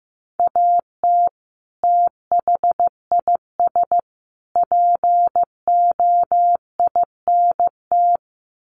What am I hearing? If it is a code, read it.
Morse code, heard as a keyed tone, about 15 wpm, 700 Hz. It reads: AT THIS POINT
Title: Sets of 3 Words 15wpm